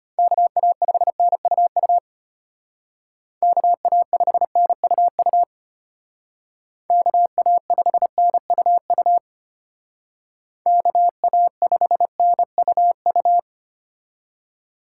Morse code, heard as a keyed tone, about 33 wpm, 700 Hz. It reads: KA5NUU KA5NUU KA5NUU KA5NUU